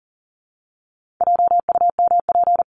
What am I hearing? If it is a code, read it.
Morse code, heard as a keyed tone, about 40 wpm, 700 Hz. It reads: JUMP
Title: Top 1251-1500 Words 40wpm